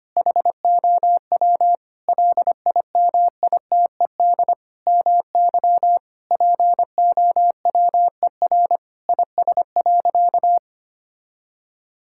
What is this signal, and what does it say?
Morse code, heard as a keyed tone, about 25 wpm, 700 Hz. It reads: HOW LIMITED MY POWER IS.
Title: Sentences from Top 1000 Words - Review Part 6 - 25wpm